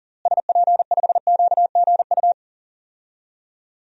Morse code, heard as a keyed tone, about 40 wpm, 700 Hz. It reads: SP5QGU
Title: DXCC-100 Call Signs - 6 Characters - Repeated 3x - 40wpm